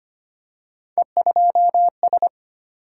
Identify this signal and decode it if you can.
Morse code, heard as a keyed tone, about 25 wpm, 700 Hz. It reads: E2S
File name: Three Letter-Number Combinations - 25wpm